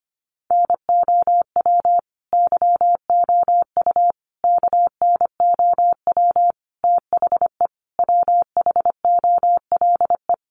Morse code, heard as a keyed tone, about 25 wpm, 700 Hz. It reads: NOW YOU KNOW THE WHOLE
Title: Sentences from Top 1200 Words 25wpm